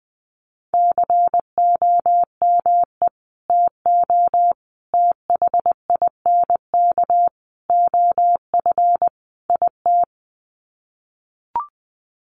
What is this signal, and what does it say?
Morse code, heard as a keyed tone, about 20 wpm, 700 Hz. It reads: COME TO THINK OF IT
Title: Sets of 5 Words Encore 20wpm